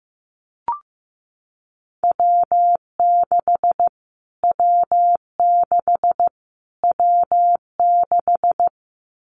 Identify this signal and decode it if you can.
Morse code, heard as a keyed tone, about 14 wpm, 700 Hz. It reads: W6 W6 W6
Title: Callsign Prefixes for Top 25 DXCC - Repeated 3x - 15wpm